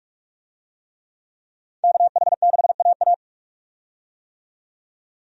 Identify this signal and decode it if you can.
Morse code, heard as a keyed tone, about 46 wpm, 700 Hz. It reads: KH6AA
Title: DXCC-100 Call Signs - 5 Characters - 45wpm